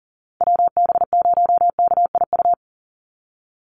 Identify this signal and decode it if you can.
Morse code, heard as a keyed tone, about 40 wpm, 700 Hz. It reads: WB0KIU
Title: NA SOTA Chaser Callsigns 40wpm